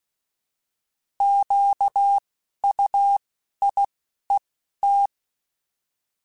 Morse code, heard as a keyed tone, about 16 wpm, 800 Hz. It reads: Q U I E T
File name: Misc / Misc-mp3s / Misc @10